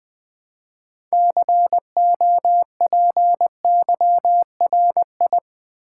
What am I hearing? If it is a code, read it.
Morse code, heard as a keyed tone, about 20 wpm, 700 Hz. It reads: COPYRI
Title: Build A Word - 3 Letters At A Time - 9-Letter Words - 20wpm